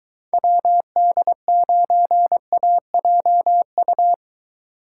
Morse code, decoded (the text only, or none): WD9AJU